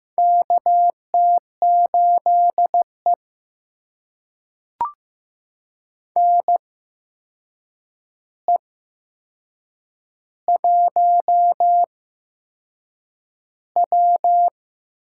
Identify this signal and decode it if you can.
Morse code, heard as a keyed tone, about 15 wpm, spaced as for 6 wpm, 700 Hz. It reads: KT8E NE1W
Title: USA Call Signs - 4 Characters - 8x Character Spacing - 15wpm